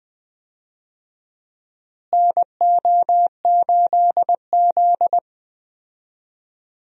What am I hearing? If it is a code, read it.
Morse code, heard as a keyed tone, about 20 wpm, 700 Hz. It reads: NO8Z
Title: USA Call Signs - 4 Characters - Repeated 2x - 20wpm